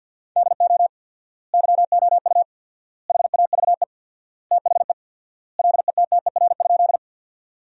Morse code, heard as a keyed tone, about 45 wpm, 700 Hz. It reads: DO YOU HAVE THE LETTER?
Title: 4x Word-Spacing - Sentences from Top 300 Words - 50wpm